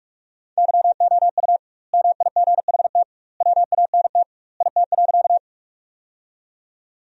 Morse code, heard as a keyed tone, about 46 wpm, 700 Hz. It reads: YOU MIGHT WANT IT.
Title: 2x Word-Spacing - Sentences from Top 300 Words - 45wpm